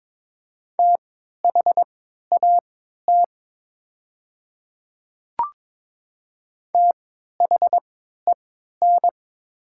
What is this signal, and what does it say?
Morse code, heard as a keyed tone, about 22 wpm, spaced as for 13 wpm, 700 Hz. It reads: THAT THEN
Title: 3x Character Spacing - Top 100 Words - 22wpm